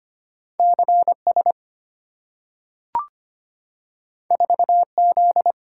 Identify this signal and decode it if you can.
Morse code, heard as a keyed tone, about 25 wpm, 690 Hz. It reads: CS 4Z
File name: Two Letter-Number Combinations - 25wpm